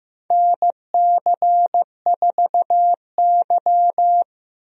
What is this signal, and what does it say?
Morse code, heard as a keyed tone, about 15 wpm, 700 Hz. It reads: NC4Y
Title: USA Call Signs - 4 Characters - Repeated 2x - 15wpm